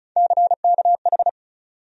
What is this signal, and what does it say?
Morse code, heard as a keyed tone, about 36 wpm, 700 Hz. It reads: CKH
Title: USA Call Sign Postfixes - 3 Letters 35wpm